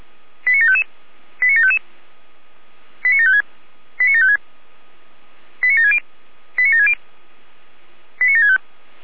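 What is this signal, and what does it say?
Five-tone selective call (ZVEI), heard as quick runs of five tones, tones of about 70 ms. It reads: 89755 89755 89756 89756 89760 89760 89765
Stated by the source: Five-tone ZVEI selective call received by SDR.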